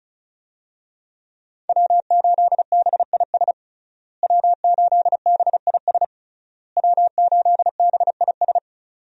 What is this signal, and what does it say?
Morse code, heard as a keyed tone, about 34 wpm, 700 Hz. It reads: W8BIS W8BIS W8BIS
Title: USA Call Signs - 5 Characters - Repeated 3x - 35wpm